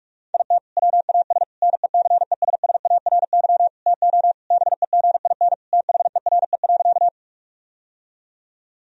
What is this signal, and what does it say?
Morse code, heard as a keyed tone, about 41 wpm, 700 Hz. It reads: IT WAS NECESSARY TO BEGIN THERE.
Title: Sentences from Top 1000 Words - Review Part 1 - 45wpm